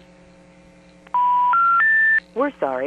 Special information tone.
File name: TurtleCreekEWSD